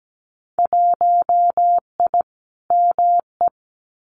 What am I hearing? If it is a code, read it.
Morse code, heard as a keyed tone, about 17 wpm, 700 Hz. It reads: 1I ME
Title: Winter Field Day Exchanges Encore 17wpm